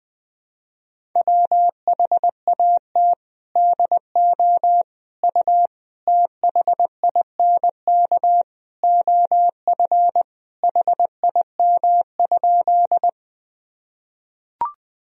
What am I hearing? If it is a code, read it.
Morse code, heard as a keyed tone, about 20 wpm, 700 Hz. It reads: WHAT DO U THINK OF HIM?
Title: Sentences from Top 200 Words and Abbreviations 20wpm